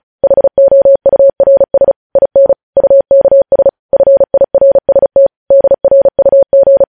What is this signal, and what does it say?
Morse code, heard as a keyed tone, about 36 wpm, 550 Hz. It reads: HOURS IN UKS FIRST DRUG
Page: News headlines in Morse code at 35 wpm